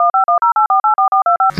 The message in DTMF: 1510848425